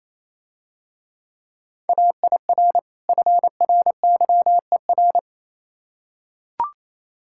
Morse code, heard as a keyed tone, about 27 wpm, 690 Hz. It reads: AIR FRYER
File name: Easy Words - Speed-Racing with Hints - 22wpm